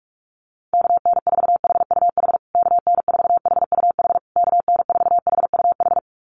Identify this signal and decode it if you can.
Morse code, heard as a keyed tone, about 44 wpm, 700 Hz. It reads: KN4HUH KN4HUH KN4HUH
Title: USA Call Signs Repeated 3x 45wpm